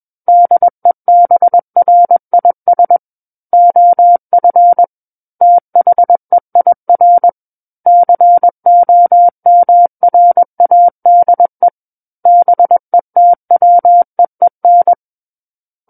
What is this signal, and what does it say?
Morse code, heard as a keyed tone, about 21 wpm, 700 Hz. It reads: DEBRIS OF THEIR COMRADE BETWEEN
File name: Book One / 12-Chapter 12 - 21 WPM